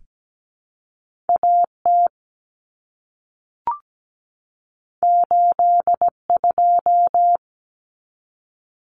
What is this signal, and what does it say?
Morse code, heard as a keyed tone, about 17 wpm, 700 Hz. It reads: AT 82
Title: Common Cut Numbers - 2-Digit 17wpm